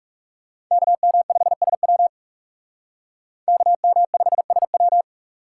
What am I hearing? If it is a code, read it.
Morse code, heard as a keyed tone, about 38 wpm, 700 Hz. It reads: KM5SW KM5SW